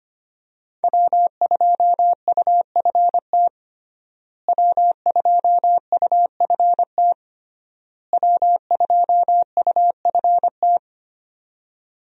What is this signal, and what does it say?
Morse code, heard as a keyed tone, about 25 wpm, 700 Hz. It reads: W2UFT W2UFT W2UFT
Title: USA Call Signs - 5 Characters - Repeated 3x - 25wpm